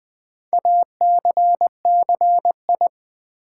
Morse code, heard as a keyed tone, about 20 wpm, 700 Hz. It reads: ACCI
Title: Build A Word - 4 Letters At A Time - 8-Letter Words - 20wpm